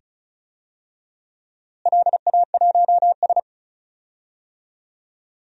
Morse code, heard as a keyed tone, about 34 wpm, 700 Hz. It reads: LA1S
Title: DXCC-100 Call Signs - 4 Characters - Repeated 2x - 35wpm